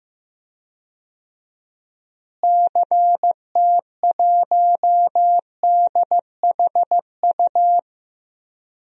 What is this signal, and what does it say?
Morse code, heard as a keyed tone, about 15 wpm, 700 Hz. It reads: CT1DHU
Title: DXCC-100 Call Signs - 6 Characters - Repeated 3x - 15wpm